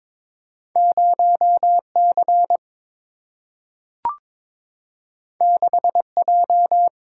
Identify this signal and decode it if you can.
Morse code, heard as a keyed tone, about 22 wpm, 700 Hz. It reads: 0C 6J
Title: Single-Letter Single-Number 22wpm